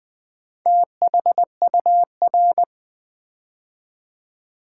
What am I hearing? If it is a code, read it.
Morse code, heard as a keyed tone, about 20 wpm, 700 Hz. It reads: THUR
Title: Build A Word - 2 Letters At A Time - 8-Letter Words - 20wpm